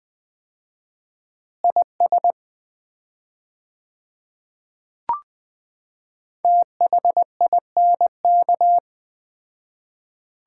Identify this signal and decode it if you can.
Morse code, heard as a keyed tone, about 20 wpm, 690 Hz. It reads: IS THINK
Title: 4x Word-Spacing - Sets of 2 Words - 20wpm